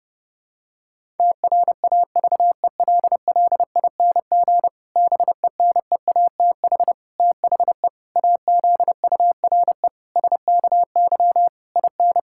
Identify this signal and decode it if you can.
Morse code, heard as a keyed tone, about 30 wpm, 700 Hz. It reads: TRAVELLING BENEATH THE AZURE SKY IN
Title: Pangram Challenge v2 - 30wpm